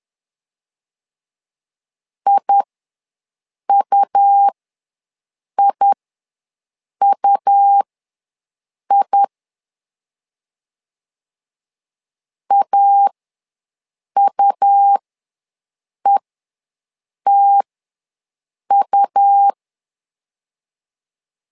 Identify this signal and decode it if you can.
Morse code, heard as a keyed tone, about 11 wpm, spaced as for 6 wpm, 800 Hz. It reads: IUIUI AUETU
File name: LEZIONI / 2 UHDN / 2 U ws 10wpm 2017